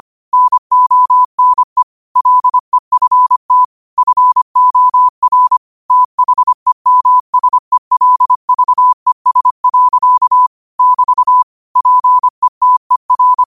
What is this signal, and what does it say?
Morse code, heard as a keyed tone, about 25 wpm, 1000 Hz. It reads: NONE LEFT FOR THEMSELVES. <BT> PETER